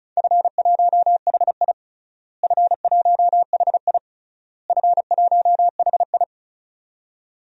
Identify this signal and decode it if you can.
Morse code, heard as a keyed tone, about 34 wpm, 700 Hz. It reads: F1HI F1HI F1HI
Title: DXCC-100 Call Signs - 4 Characters - Repeated 3x - 35wpm